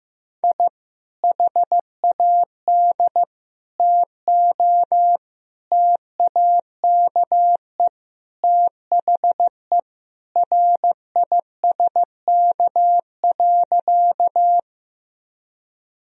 Morse code, heard as a keyed tone, about 15 wpm, 700 Hz. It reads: I HAD TO TAKE THE RISK.
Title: Sentences from Top 1000 Words - Review Part 4 - 15wpm